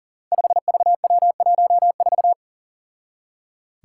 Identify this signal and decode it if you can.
Morse code, heard as a keyed tone, about 39 wpm, 700 Hz. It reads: 5VW14
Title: Five Challenging Letter-Number Combinations 40wpm